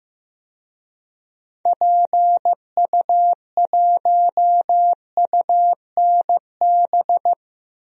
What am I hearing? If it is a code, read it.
Morse code, heard as a keyed tone, about 15 wpm, 700 Hz. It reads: PU1UNB
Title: DXCC-100 Call Signs - 6 Characters - Repeated 2x - 15wpm